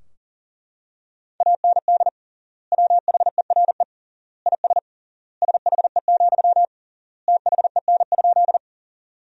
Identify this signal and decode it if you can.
Morse code, heard as a keyed tone, about 40 wpm, 700 Hz. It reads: AND WHERE IS SHE, THEN?
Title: 3x Word-Spacing - Sentences from Top 200 Words - 40wpm